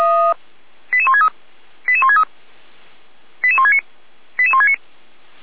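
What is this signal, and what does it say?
Five-tone selective call (ZVEI), heard as quick runs of five tones, tones of about 70 ms. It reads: 88163 88163 88169 88169
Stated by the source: ZVEI five-tone selective call received on an SDR.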